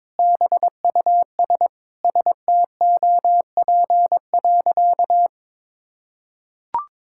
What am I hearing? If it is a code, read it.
Morse code, heard as a keyed tone, about 21 wpm, 700 Hz. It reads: BUS STOP.
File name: practice / 22wpm / Sentences with 2-Word Phrases - 22wpm